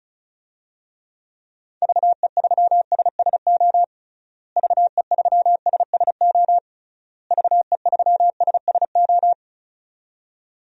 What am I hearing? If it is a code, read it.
Morse code, heard as a keyed tone, about 36 wpm, 700 Hz. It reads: VE3SSO VE3SSO VE3SSO
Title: DXCC-100 Call Signs - 6 Characters - Repeated 3x - 35wpm